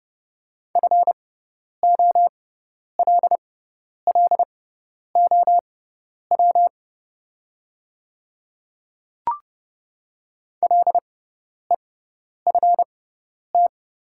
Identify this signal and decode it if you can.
Morse code, heard as a keyed tone, about 30 wpm, spaced as for 10 wpm, 700 Hz. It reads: FOLLOW LEFT
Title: 6x Character Spacing - Top 300 Words - 30wpm